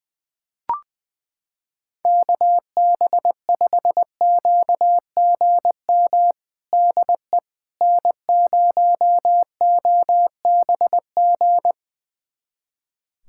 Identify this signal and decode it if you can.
Morse code, heard as a keyed tone, about 20 wpm, 700 Hz. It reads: KB5QGM DE N0OBG
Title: USA Callsign DE Callsign 20wpm